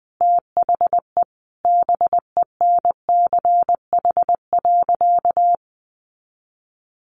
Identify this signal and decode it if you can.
Morse code, heard as a keyed tone, about 20 wpm, 700 Hz. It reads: THE BENCH.
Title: Sentences from Top 1400 Words 20wpm